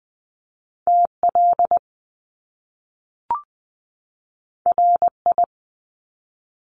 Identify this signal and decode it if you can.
Morse code, heard as a keyed tone, about 20 wpm, 700 Hz. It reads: TL RI
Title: Top Two Letter Combinations in Words 20wpm